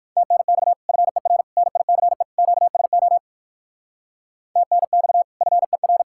Morse code, heard as a keyed tone, about 58 wpm, 700 Hz. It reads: TNX FER NICE QSO TNX FER